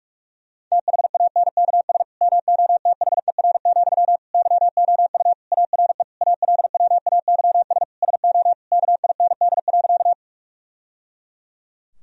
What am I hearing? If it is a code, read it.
Morse code, heard as a keyed tone, about 46 wpm, 700 Hz. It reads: THANKS MOTHER, YOU ARE ALWAYS SO KIND.
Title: Sentences from Top 1000 Words - Review Part 3 - 45wpm